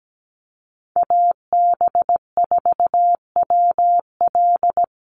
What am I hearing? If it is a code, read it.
Morse code, heard as a keyed tone, about 17 wpm, 700 Hz. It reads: AB4WL
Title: NA SOTA Chaser Callsigns 17wpm